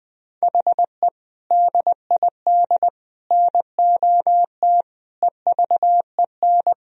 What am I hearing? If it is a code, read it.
Morse code, heard as a keyed tone, about 19 wpm, 700 Hz. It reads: HE DID NOT EVEN
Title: Sentences from Top 1000 Words - Review Part 1 - 20wpm